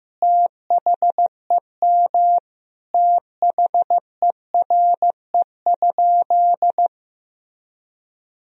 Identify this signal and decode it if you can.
Morse code, heard as a keyed tone, about 15 wpm, 700 Hz. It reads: THEM THERE?
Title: Sentences from Top 1000 Words - Review Part 2 - 15wpm